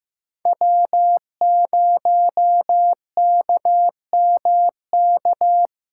Morse code, heard as a keyed tone, about 16 wpm, 700 Hz. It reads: W0KMK
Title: USA Callsign DE Callsign - 5 Characters - 15wpm